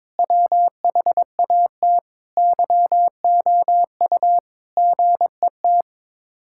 Morse code, heard as a keyed tone, about 22 wpm, 700 Hz. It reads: WHAT YOU GET
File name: practice / 22wpm / Sets of 7 Words - 22wpm